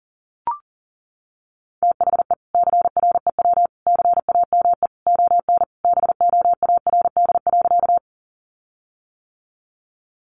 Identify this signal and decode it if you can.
Morse code, heard as a keyed tone, about 40 wpm, 700 Hz. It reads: THE CREW CAME ON BOARD.
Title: Sentences from Top 1500 Words 40wpm